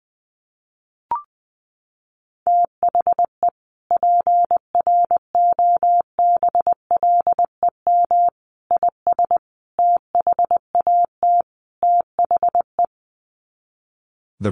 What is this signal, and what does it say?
Morse code, heard as a keyed tone, about 20 wpm, 700 Hz. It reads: THE PROBLEM IS THAT THE
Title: Sets of 5 Words Encore2 20wpm